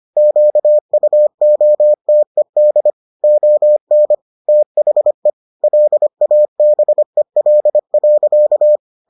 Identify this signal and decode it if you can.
Morse code, heard as a keyed tone, about 25 wpm, 600 Hz. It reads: QUOTED ON THE LABEL.